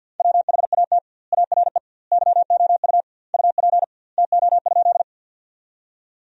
Morse code, heard as a keyed tone, about 52 wpm, 700 Hz. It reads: WHAT ARE YOU UP TO?
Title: 2x Word-Spacing - Sentences from Top 100 Words - 50wpm